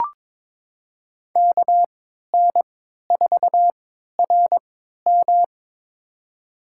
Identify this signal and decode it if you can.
Morse code, heard as a keyed tone, about 22 wpm, spaced as for 13 wpm, 700 Hz. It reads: KN4RM
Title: USA Call Signs - 3x Character Spacing - 22wpm